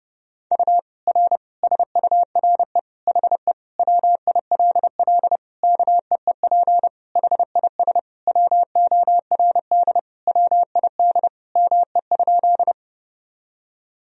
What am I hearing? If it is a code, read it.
Morse code, heard as a keyed tone, about 30 wpm, 700 Hz. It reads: U R SURE HE WILL KEEP HIS WORD WID ME?
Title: Sentences from Top 300 Words and Abbreviations 30wpm